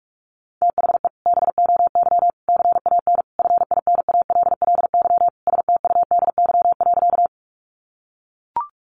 Morse code, heard as a keyed tone, about 44 wpm, 700 Hz. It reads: THE BOY CAN FINALLY STUDY.
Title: Additional Easy Phrases 45wpm